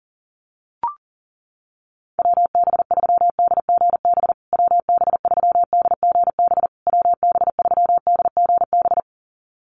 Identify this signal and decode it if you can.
Morse code, heard as a keyed tone, about 40 wpm, 700 Hz. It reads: WB3DGB WB3DGB WB3DGB
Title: USA Call Signs Repeated 3x 40wpm